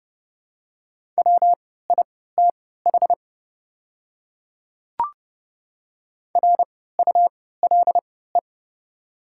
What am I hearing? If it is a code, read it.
Morse code, heard as a keyed tone, about 30 wpm, spaced as for 17 wpm, 700 Hz. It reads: WITH RULE
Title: 3x Character Spacing - Top 500 Words - 30wpm